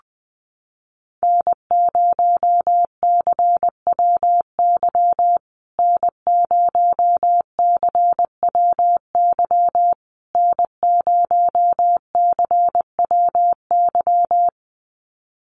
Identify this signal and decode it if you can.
Morse code, heard as a keyed tone, about 20 wpm, 700 Hz. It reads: N0CWY N0CWY N0CWY
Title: USA Call Signs Repeated 3x 20wpm